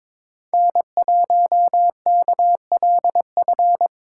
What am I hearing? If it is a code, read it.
Morse code, heard as a keyed tone, about 26 wpm, 700 Hz. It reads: N1KLF
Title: Speed Racing - USA 1x3 Call Signs 22wpm